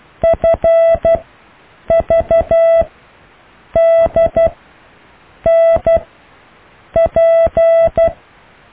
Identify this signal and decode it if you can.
Morse code, heard as a keyed tone, about 12 wpm, spaced as for 7 wpm, 660 Hz. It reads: FVDNP
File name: cw / EKTPV+BFHNW